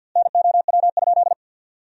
Morse code, heard as a keyed tone, about 50 wpm, 700 Hz. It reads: NOW?
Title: 4x Word-Spacing - Sentences from Top 100 Words - 50wpm